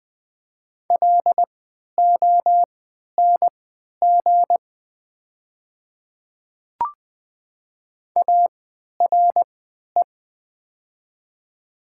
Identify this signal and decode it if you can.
Morse code, heard as a keyed tone, about 20 wpm, spaced as for 11 wpm, 700 Hz. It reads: LONG ARE
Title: 3x Character Spacing - Top 200 Words - 20wpm